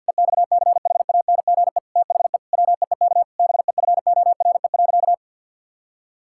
Morse code, heard as a keyed tone, about 50 wpm, 700 Hz. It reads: EXCHANGE THE WEEK BEFORE.
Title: Sentences from Top 1000 Words - Review Part 6 - 50wpm